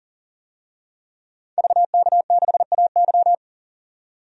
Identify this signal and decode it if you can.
Morse code, heard as a keyed tone, about 40 wpm, 700 Hz. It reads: VK6AY
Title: DXCC-100 Call Signs - 5 Characters - 40wpm